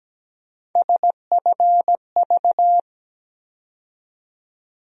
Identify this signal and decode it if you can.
Morse code, heard as a keyed tone, about 17 wpm, 700 Hz. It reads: SFV
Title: USA Call Sign Postfixes - 3 Letters 17wpm